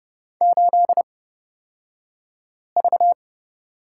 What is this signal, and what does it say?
Morse code, heard as a keyed tone, about 31 wpm, spaced as for 5 wpm, 700 Hz. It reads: 8V
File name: Lesson 38d - Letter q - Two Characters 30wpm